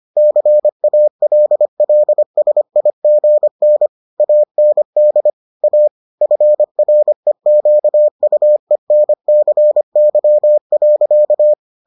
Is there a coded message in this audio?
Yes: Morse code from 0.2 to 11.5 s, heard as a keyed tone, about 25 wpm, 600 Hz. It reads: CALLSIGN AND A FREQUENCY.